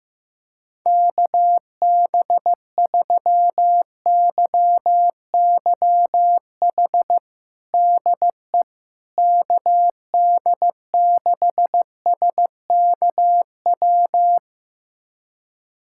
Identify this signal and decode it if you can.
Morse code, heard as a keyed tone, about 16 wpm, 700 Hz. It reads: KB3YYH DE KD6SKW
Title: USA Callsign DE Callsign - 6 Characters - 15wpm